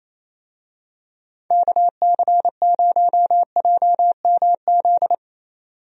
Morse code, heard as a keyed tone, about 28 wpm, 700 Hz. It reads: KC0JMZ
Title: USA Call Signs - 6 Characters - 28wpm